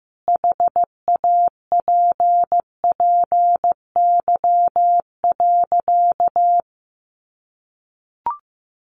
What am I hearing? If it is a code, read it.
Morse code, heard as a keyed tone, about 15 wpm, 700 Hz. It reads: HAPPY.
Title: Sentences from Top 1000 Words 15wpm